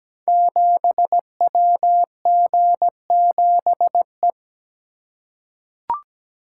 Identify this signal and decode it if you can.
Morse code, heard as a keyed tone, about 17 wpm, 700 Hz. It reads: 7WG7E
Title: Code Groups - 5 Letter:Numbers - 17wpm